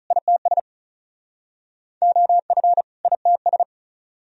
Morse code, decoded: ITS OF ITS

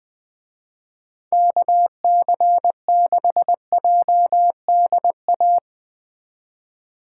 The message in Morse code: KC6JDA

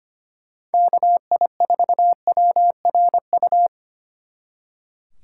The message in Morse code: KI4WRU

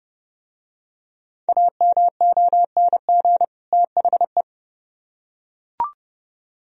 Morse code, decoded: AMONG THE

